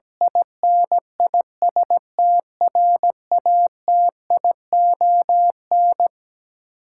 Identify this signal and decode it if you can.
Morse code, heard as a keyed tone, about 17 wpm, 700 Hz. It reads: INISTRATION
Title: Sentences from Top 1300 Words - New Word Drill - 17wpm